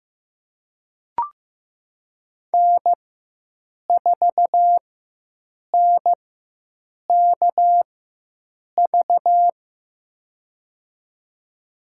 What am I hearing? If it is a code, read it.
Morse code, heard as a keyed tone, about 16 wpm, spaced as for 7 wpm, 700 Hz. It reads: N4NKV
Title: USA Call Signs - 5 Characters - 4x Character Spacing - 15wpm